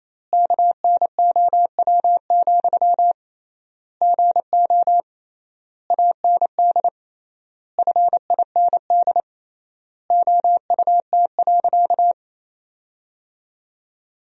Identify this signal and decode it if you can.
Morse code, heard as a keyed tone, about 28 wpm, 700 Hz. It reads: KNOW, GO AND FIND OUT.
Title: 3x Word-Spacing - Sentences from Top 200 Words - 28wpm